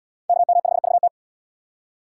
Morse code, heard as a keyed tone, about 74 wpm, 700 Hz. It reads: KM5VT